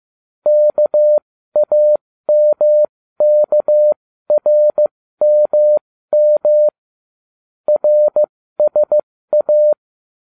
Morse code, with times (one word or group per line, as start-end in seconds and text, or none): 0.5-6.7 KAMKRMM
7.7-9.7 RSA